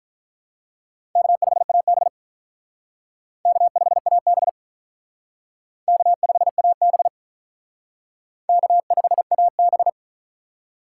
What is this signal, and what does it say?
Morse code, heard as a keyed tone, about 46 wpm, 700 Hz. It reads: K5AB K5AB K5AB K5AB